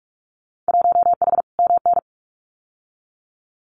Morse code, heard as a keyed tone, about 46 wpm, 700 Hz. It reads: 1H MN
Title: Winter Field Day Exchanges 45wpm